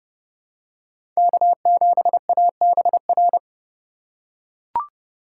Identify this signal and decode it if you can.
Morse code, heard as a keyed tone, about 30 wpm, 700 Hz. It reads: K7ABR
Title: USA Call Signs - 5 Characters - 3x Character Spacing - 30wpm